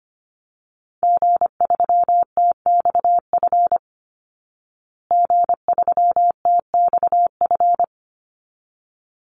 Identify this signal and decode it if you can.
Morse code, heard as a keyed tone, about 25 wpm, 700 Hz. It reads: G3TXF G3TXF
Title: Call Signs Repeated 3x - 25wpm